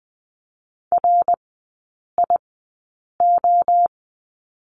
Morse code, heard as a keyed tone, about 19 wpm, spaced as for 8 wpm, 690 Hz. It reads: RIO
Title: 060 Lesson 11e - Letter h - Three Characters 20wpm